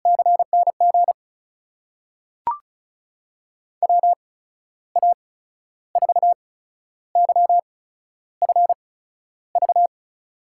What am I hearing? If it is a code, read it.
Morse code, heard as a keyed tone, about 36 wpm, spaced as for 12 wpm, 700 Hz. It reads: CNG WA4YFV
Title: USA Call Signs - 6 Characters - 8x Character Spacing - 35wpm